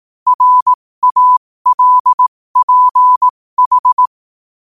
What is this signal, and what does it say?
Morse code, heard as a keyed tone, about 18 wpm, spaced as for 15 wpm, 1000 Hz. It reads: RALPH
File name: qotdfiles / 15wpm